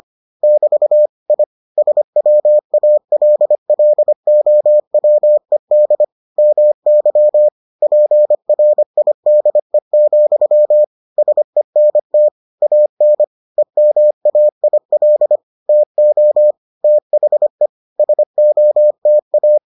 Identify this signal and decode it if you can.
Morse code, heard as a keyed tone, about 25 wpm, 600 Hz. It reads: <BT> I SWALLOWED MY PRIDE, SENT AN EMAIL TO THE SOTA